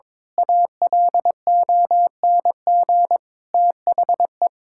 Morse code, heard as a keyed tone, about 22 wpm, 700 Hz. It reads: ALONG THE
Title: Pangram Challenge v2 - 22wpm